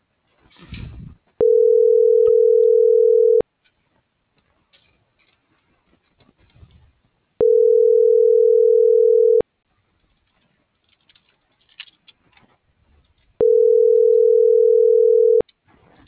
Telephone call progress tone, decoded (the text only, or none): ringback tone